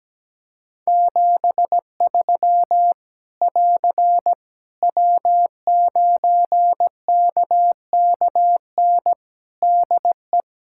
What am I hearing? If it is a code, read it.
Morse code, heard as a keyed tone, about 17 wpm, 700 Hz. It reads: 73 <AR> W9KKN DE
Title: Practice QSO - Exchange 4 - Ending - 17wpm